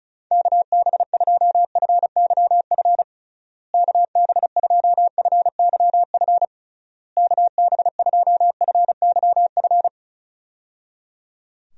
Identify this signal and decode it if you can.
Morse code, heard as a keyed tone, about 36 wpm, 700 Hz. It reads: KB2FYF KB2FYF KB2FYF
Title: USA Call Signs - 6 Characters - Repeated 3x - 35wpm